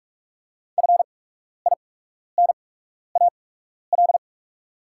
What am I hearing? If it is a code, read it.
Morse code, heard as a keyed tone, about 45 wpm, spaced as for 12 wpm, 700 Hz. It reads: FINAL
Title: 8x Character Spacing - Top 500 Words - 45wpm